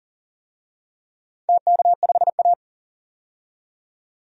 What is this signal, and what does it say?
Morse code, heard as a keyed tone, about 42 wpm, 700 Hz. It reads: TK5A
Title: DXCC-100 Call Signs - 4 Characters - 40wpm